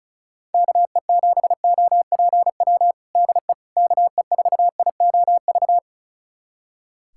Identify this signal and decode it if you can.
Morse code, heard as a keyed tone, about 34 wpm, 700 Hz. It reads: KE7OPW DE KE4IOV
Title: USA Callsign DE Callsign - 6 Characters - 35wpm